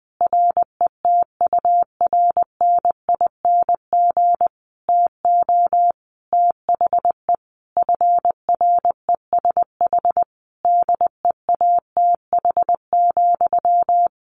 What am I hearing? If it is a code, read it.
Morse code, heard as a keyed tone, about 20 wpm, 700 Hz. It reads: RETURNING TO THE FRESH DEATH,
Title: A Sound Of Thunder 20wpm